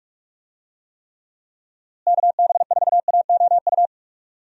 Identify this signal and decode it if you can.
Morse code, heard as a keyed tone, about 46 wpm, 700 Hz. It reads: KB4AOU
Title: USA Call Signs - 6 Characters - Repeated 3x - 45wpm